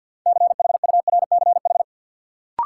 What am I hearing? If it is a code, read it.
Morse code, heard as a keyed tone, about 52 wpm, 700 Hz. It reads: CHURCH
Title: Sets of 5 Words - New Word Drill - 50wpm